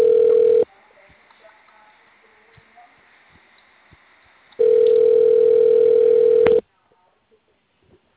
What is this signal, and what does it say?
Ringback tone (North America), 440 + 480 Hz, with speech.
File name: out- -2003- - - .147